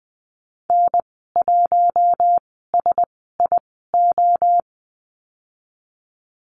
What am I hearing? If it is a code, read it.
Morse code, heard as a keyed tone, about 20 wpm, spaced as for 15 wpm, 700 Hz. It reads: N1SIO
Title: 055 Lesson 10g - Letter r - Call Signs 20wpm